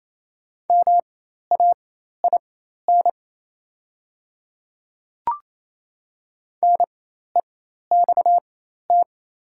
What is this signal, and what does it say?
Morse code, heard as a keyed tone, about 28 wpm, spaced as for 13 wpm, 700 Hz. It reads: MAIN NEXT
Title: 4x Character Spacing - Top 300 Words - 28wpm